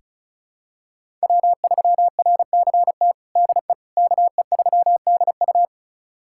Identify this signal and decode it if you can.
Morse code, heard as a keyed tone, about 36 wpm, 700 Hz. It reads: W3RCT DE KE3DU
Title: USA Callsign DE Callsign - 5 Characters - 35wpm